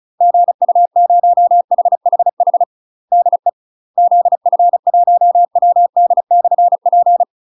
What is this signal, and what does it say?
Morse code, heard as a keyed tone, about 35 wpm, 700 Hz. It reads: GU0HHH DE ZF1WD/P